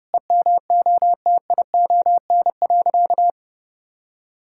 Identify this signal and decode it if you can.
Morse code, heard as a keyed tone, about 30 wpm, 700 Hz. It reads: EMOTION.
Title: Sentences from Top 1600 Words 30wpm